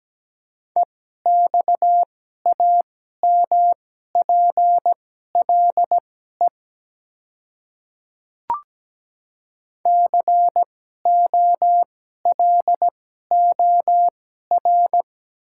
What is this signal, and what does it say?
Morse code, heard as a keyed tone, about 17 wpm, spaced as for 12 wpm, 700 Hz. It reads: EXAMPLE COLOR
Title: 2x Character Spacing - Top 300 Words - 17wpm